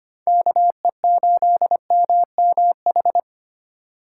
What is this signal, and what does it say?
Morse code, heard as a keyed tone, about 25 wpm, 700 Hz. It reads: KE8MMH
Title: USA Call Signs - 6 Characters - 3x Character Spacing - 25wpm